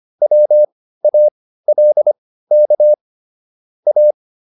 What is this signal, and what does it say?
Morse code, heard as a keyed tone, about 25 wpm, spaced as for 15 wpm, 600 Hz. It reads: WALK A
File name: PR302 15